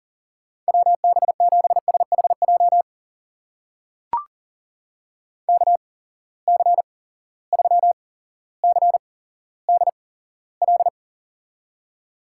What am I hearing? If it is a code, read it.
Morse code, heard as a keyed tone, about 40 wpm, spaced as for 17 wpm, 700 Hz. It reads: WB7SHJ KC3CDL
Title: USA Call Signs - 6 Characters - 8x Character Spacing - 40wpm